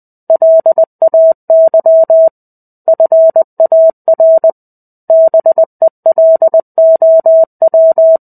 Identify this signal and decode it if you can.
Morse code, heard as a keyed tone, about 20 wpm, 650 Hz. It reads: LAY FAR BELOW
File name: 05-Chapter-05 20 WPM